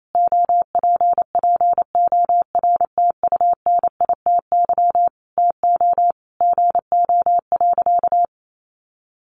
Morse code, heard as a keyed tone, about 28 wpm, 700 Hz. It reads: OPPORTUNITY TO GO.
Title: Sentences from Top 1000 Words 28wpm